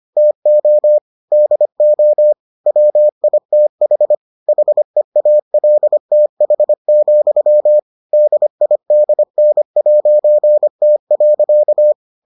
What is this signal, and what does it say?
Morse code, heard as a keyed tone, about 25 wpm, 600 Hz. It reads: TO DO WITH HEALTH, DIDN'T.